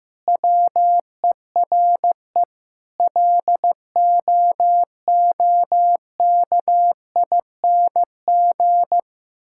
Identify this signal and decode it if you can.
Morse code, heard as a keyed tone, about 15 wpm, 700 Hz. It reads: WERE LOOKING
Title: Sentences from Top 1000 Words - Review Part 6 - 15wpm